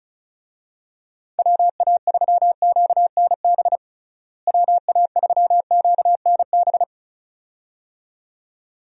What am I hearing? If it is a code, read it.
Morse code, heard as a keyed tone, about 34 wpm, 700 Hz. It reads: WA3QNB WA3QNB
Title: USA Call Signs - 6 Characters - Repeated 2x - 35wpm